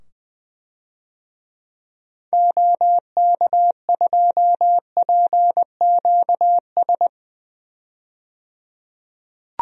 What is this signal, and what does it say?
Morse code, heard as a keyed tone, about 20 wpm, 700 Hz. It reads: OK2PQS
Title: DXCC-100 Call Signs - 6 Characters - 20wpm